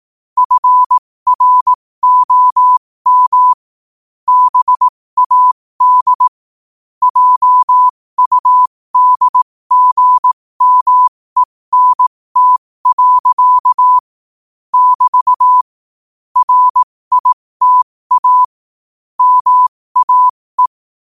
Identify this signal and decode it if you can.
Morse code, heard as a keyed tone, about 18 wpm, spaced as for 15 wpm, 1000 Hz. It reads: FROM BAD JUDGMENT. <BT> RITA MAE
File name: qotdfiles / 15wpm